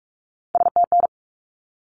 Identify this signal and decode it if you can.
Morse code, heard as a keyed tone, about 44 wpm, 700 Hz. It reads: STN
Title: Abbreviations 45wpm